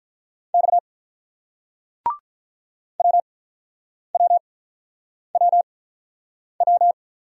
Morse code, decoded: X WWWW

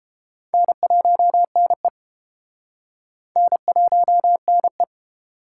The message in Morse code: N1NE N1NE

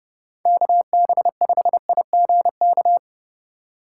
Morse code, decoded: KB5IGK

